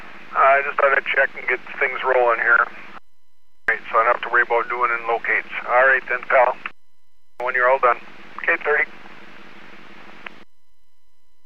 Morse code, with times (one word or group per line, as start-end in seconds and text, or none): none